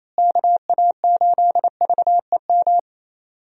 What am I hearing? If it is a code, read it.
Morse code, heard as a keyed tone, about 28 wpm, 700 Hz. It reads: KA8VEM
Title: USA Call Signs - 6 Characters - Repeated 2x - 28wpm